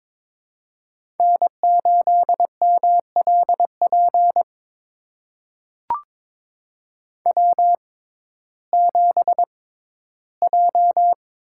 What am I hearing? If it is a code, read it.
Morse code, heard as a keyed tone, about 22 wpm, spaced as for 14 wpm, 700 Hz. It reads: N8MLP W7J
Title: USA Call Signs - 5 Characters - 6x Character Spacing - 22wpm